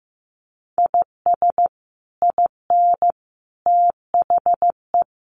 Morse code, heard as a keyed tone, about 15 wpm, 700 Hz. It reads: IS IN THE
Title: Sets of 5 Words Encore2 15wpm